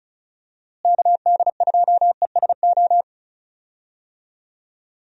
Morse code, heard as a keyed tone, about 34 wpm, 700 Hz. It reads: KD2ESO
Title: USA Call Signs - 6 Characters - Repeated 3x - 35wpm